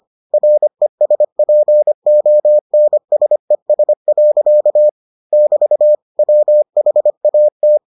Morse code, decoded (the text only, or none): RESPONSES. <BT> WHAT